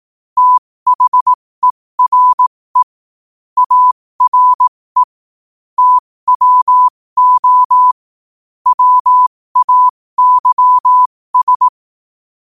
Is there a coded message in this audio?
Yes: Morse code from 0.4 to 11.7 s, heard as a keyed tone, about 18 wpm, spaced as for 15 wpm, 1000 Hz. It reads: THERE ARE TWO WAYS